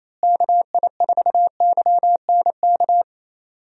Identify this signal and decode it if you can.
Morse code, heard as a keyed tone, about 27 wpm, 700 Hz. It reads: KI4YNK
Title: USA Call Signs - 6 Characters - Repeated 3x - 28wpm